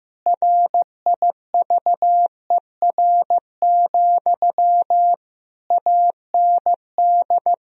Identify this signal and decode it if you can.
Morse code, heard as a keyed tone, about 15 wpm, 700 Hz. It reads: RIVER, AND
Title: Sentences from Top 1000 Words - Review Part 1 - 15wpm